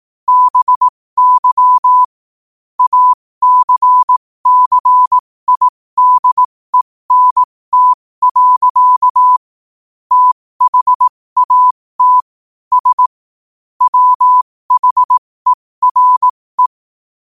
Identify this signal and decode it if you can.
Morse code, heard as a keyed tone, about 18 wpm, spaced as for 15 wpm, 1000 Hz. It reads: BY ACCIDENT. THAT S WHERE